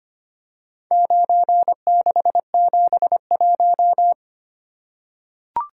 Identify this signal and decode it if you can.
Morse code, heard as a keyed tone, about 25 wpm, 700 Hz. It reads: 9671
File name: practice / 25wpm / Four Letter-Number Combinations - 25wpm